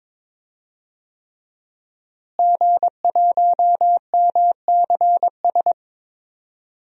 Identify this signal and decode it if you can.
Morse code, heard as a keyed tone, about 22 wpm, 700 Hz. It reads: G1MCS
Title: DXCC-100 Call Signs - 5 Characters - Repeated 3x - 22wpm